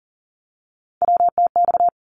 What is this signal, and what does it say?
Morse code, heard as a keyed tone, about 40 wpm, 700 Hz. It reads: WTX
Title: ARRL Sections 40wpm